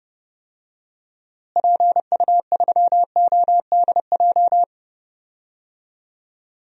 Morse code, heard as a keyed tone, about 29 wpm, 700 Hz. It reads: PU3ODJ
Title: DXCC-100 Call Signs - 6 Characters - Repeated 3x - 30wpm